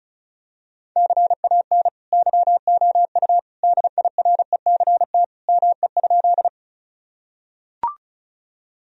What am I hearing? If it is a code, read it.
Morse code, heard as a keyed tone, about 36 wpm, 700 Hz. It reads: CAN YOU DIRECT ME?
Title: Sentences from Top 1000 Words - Review Part 2 - 35wpm